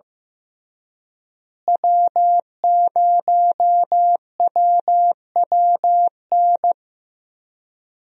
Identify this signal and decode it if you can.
Morse code, heard as a keyed tone, about 15 wpm, 700 Hz. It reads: W0WWN
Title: USA Call Signs - 5 Characters - Repeated 3x - 15wpm